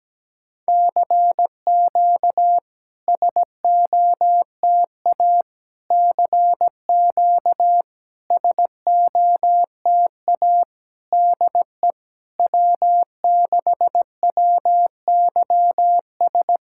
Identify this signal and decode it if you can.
Morse code, heard as a keyed tone, about 17 wpm, 700 Hz. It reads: CQ SOTA CQ SOTA DE W6WYS
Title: SOTA QSO Practice 17wpm